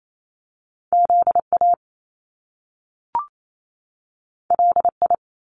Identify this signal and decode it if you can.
Morse code, heard as a keyed tone, about 28 wpm, 700 Hz. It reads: Z A L I
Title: Top Two Letter Combinations in Words 28wpm